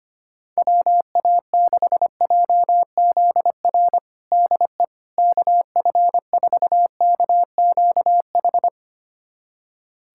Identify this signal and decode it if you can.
Morse code, heard as a keyed tone, about 25 wpm, 700 Hz. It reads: WA6JZR DE KF4KQH
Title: USA Callsign DE Callsign - 6 Characters - 25wpm